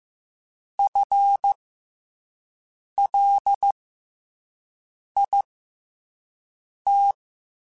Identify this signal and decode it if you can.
Morse code, heard as a keyed tone, about 15 wpm, spaced as for 5 wpm, 800 Hz. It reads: FLIT
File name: FL / FL mp3s / FL Prefixes 3x2 sp @5